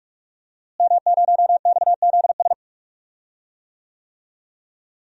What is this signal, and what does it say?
Morse code, heard as a keyed tone, about 41 wpm, 700 Hz. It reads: M0XZS
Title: DXCC-100 Call Signs - 5 Characters - Repeated 2x - 45wpm